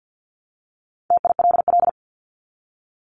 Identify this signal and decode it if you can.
Morse code, heard as a keyed tone, about 50 wpm, 700 Hz. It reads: TILL
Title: Top 1751-2000 Words 50wpm